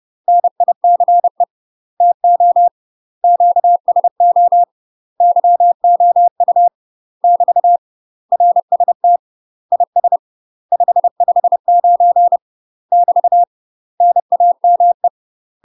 Morse code, heard as a keyed tone, about 30 wpm, 700 Hz. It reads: NICE TO QSO YOU <BT> RST IS 559 <BT> NAME
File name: UK MASTER QSOs TRAINING / 30WPM / UK MASTER QSOs 0045